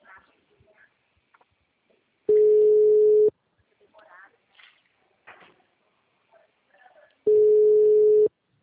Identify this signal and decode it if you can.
Ringback tone (Europe, 425 Hz), with speech.
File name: out- -2009- - - .12596